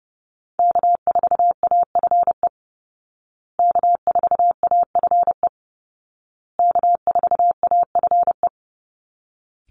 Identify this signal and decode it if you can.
Morse code, heard as a keyed tone, about 30 wpm, 700 Hz. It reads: K4AFE K4AFE K4AFE
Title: Call Signs Repeated 2x - 30wpm